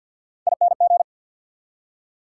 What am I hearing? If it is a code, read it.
Morse code, heard as a keyed tone, about 52 wpm, 690 Hz. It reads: ING
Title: Prefixes and Suffixes 50wpm